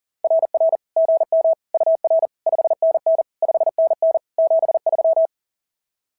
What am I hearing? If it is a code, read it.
Morse code, heard as a keyed tone, about 40 wpm, 590 Hz. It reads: RR GM UR 5NN 5NN 73
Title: SOTA QSO Practice 40wpm